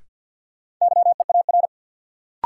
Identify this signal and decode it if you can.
Morse code, heard as a keyed tone, about 53 wpm, 700 Hz. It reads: YEAR
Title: Sentences from Top 200 Words - New Word Drill - 50wpm